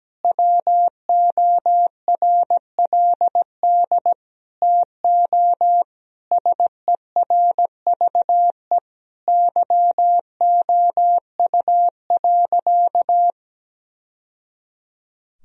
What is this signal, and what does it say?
Morse code, heard as a keyed tone, about 17 wpm, 700 Hz. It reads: WORLD TO SERVE YOU.
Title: Sentences from Top 1000 Words - Review Part 2 - 17wpm